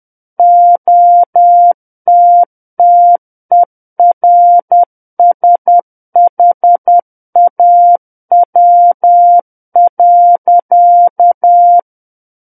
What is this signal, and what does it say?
Morse code, heard as a keyed tone, about 10 wpm, 700 Hz. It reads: OTTERSHAW.
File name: Book One / 01-Chapter 1 - 10 WPM